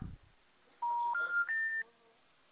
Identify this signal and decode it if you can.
Special information tone.
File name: out- -2020- - - .746